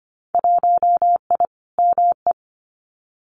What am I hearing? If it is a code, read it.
Morse code, heard as a keyed tone, about 25 wpm, 700 Hz. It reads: 1I ME
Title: Winter Field Day Exchanges 25wpm